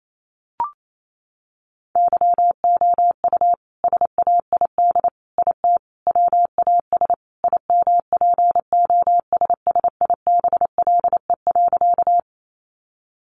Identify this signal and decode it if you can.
Morse code, heard as a keyed tone, about 28 wpm, 700 Hz. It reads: YOU SAID IT WAS IMPOSSIBLE.
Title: Sentences from Top 1100 Words 28wpm